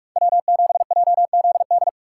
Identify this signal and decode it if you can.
Morse code, heard as a keyed tone, about 43 wpm, 700 Hz. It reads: W7JZD
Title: USA Call Signs - 5 Characters - 4x Character Spacing - 45wpm